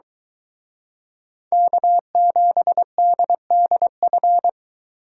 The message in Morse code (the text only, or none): K7DDF